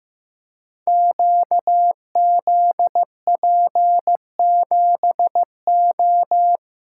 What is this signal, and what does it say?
Morse code, heard as a keyed tone, about 15 wpm, 700 Hz. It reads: QZP7O
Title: Code Groups - 5 Letter:Numbers - 15wpm